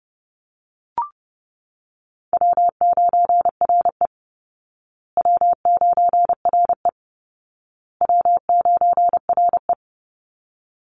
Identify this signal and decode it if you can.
Morse code, heard as a keyed tone, about 30 wpm, 700 Hz. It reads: W9RE W9RE W9RE
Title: Call Signs Repeated 2x - 30wpm